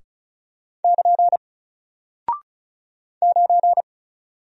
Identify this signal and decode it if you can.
Morse code, heard as a keyed tone, about 36 wpm, 700 Hz. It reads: <KN> 9
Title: Single Character (Letter:Number:Punctuation:Prosign) Alt 35wpm